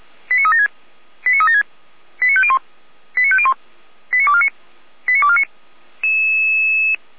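Five-tone selective call (ZVEI), heard as quick runs of five tones, tones of about 70 ms. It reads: 89376 89376 89501 89501 89249 89249
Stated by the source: ZVEI five-tone selective call 89163 received on SDR.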